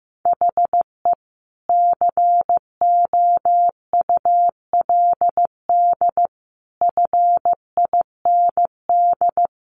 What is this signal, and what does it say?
Morse code, heard as a keyed tone, about 15 wpm, 700 Hz. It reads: HE COULD FIND
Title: Sentences from Top 1400 Words 15wpm